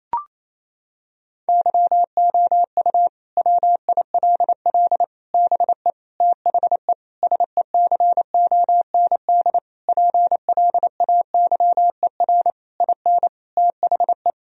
Morse code, heard as a keyed tone, about 28 wpm, 700 Hz. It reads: YOU WILL BE THE SECOND PLAYER IN THE
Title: Sentences from Top 1000 Words - Review Part 4 - 28wpm